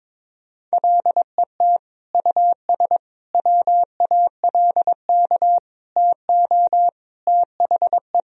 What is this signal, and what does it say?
Morse code, heard as a keyed tone, about 22 wpm, 700 Hz. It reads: LET US WALK TO THE
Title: Sentences from Top 1000 Words - Review Part 2 - 22wpm